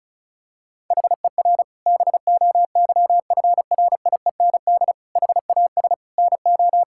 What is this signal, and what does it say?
Morse code, heard as a keyed tone, about 34 wpm, 700 Hz. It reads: HER BOYFRIEND HAS NO
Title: Sentences from Top 1000 Words - Review Part 5 - 35wpm